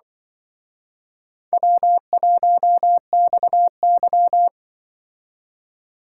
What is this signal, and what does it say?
Morse code, heard as a keyed tone, about 24 wpm, 700 Hz. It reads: W1XY
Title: Speed Racing - USA 1x2 Call Signs 20wpm